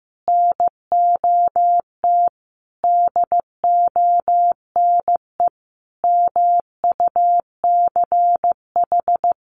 Morse code, heard as a keyed tone, about 15 wpm, 700 Hz. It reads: NOT DONE MUCH
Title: Sentences from Top 1100 Words 15wpm